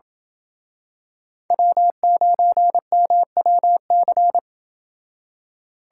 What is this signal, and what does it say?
Morse code, heard as a keyed tone, about 28 wpm, 700 Hz. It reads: W9MWC